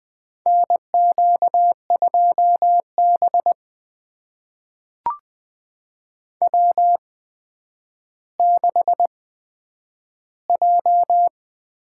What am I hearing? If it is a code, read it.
Morse code, heard as a keyed tone, about 20 wpm, spaced as for 10 wpm, 700 Hz. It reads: NQ2B W6J
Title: USA Call Signs - 8x Character Spacing - 20wpm